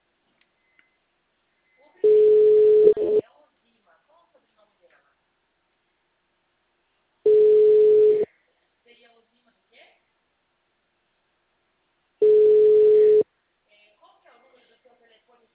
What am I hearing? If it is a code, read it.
Ringback tone (Europe, 425 Hz), with speech.
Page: Marilia - SP